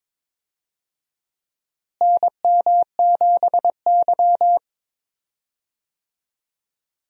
Morse code, heard as a keyed tone, about 22 wpm, 700 Hz. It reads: NM7Y
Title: USA Call Signs - 4 Characters - Repeated 2x - 22wpm